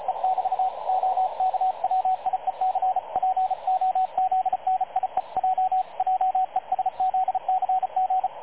Morse code, heard as a keyed tone, about 35 wpm, 750 Hz, with radio noise. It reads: E <BT> OK WIEC POZNIEJ JESZCG
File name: dlazaawansowanych / nagraniazpasma / websdr recording 3567.0kHz